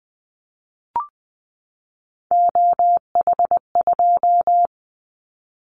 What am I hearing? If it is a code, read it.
Morse code, heard as a keyed tone, about 20 wpm, 700 Hz. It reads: OH2
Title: Callsign Prefixes for Top 50 DXCC - 20wpm